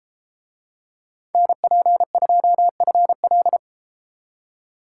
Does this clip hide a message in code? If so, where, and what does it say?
Yes: Morse code from 1.3 to 3.6 s, heard as a keyed tone, about 30 wpm, 700 Hz. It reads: NP2FL